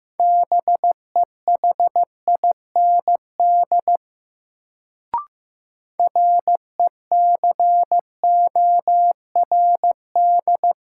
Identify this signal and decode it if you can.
Morse code, heard as a keyed tone, about 15 wpm, 700 Hz. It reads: BEHIND RECORD
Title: Sentences from Top 500 Words - New Word Drill - 15wpm